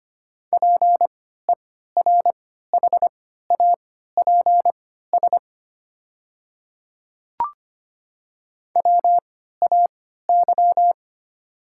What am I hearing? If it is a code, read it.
Morse code, heard as a keyed tone, about 25 wpm, spaced as for 14 wpm, 690 Hz. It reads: PERHAPS WAY
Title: 3x Character Spacing - Top 500 Words - 25wpm